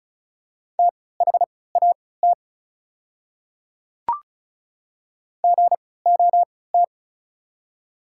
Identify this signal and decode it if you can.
Morse code, heard as a keyed tone, about 34 wpm, spaced as for 20 wpm, 700 Hz. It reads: THAT GOT
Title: 3x Character Spacing - Top 300 Words - 35wpm